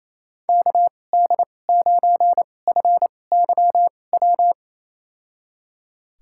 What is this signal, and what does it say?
Morse code, heard as a keyed tone, about 28 wpm, spaced as for 20 wpm, 700 Hz. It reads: KD9FYW
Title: USA Call Signs - 6 Characters - 2x Character Spacing - 28wpm